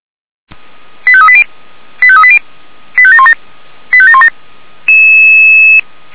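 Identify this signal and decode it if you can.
Five-tone selective call (ZVEI), heard as quick runs of five tones, tones of about 70 ms. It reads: 85380 85380 85717 85717
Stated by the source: Five-tone ZVEI selective call, code 85909, received by SDR.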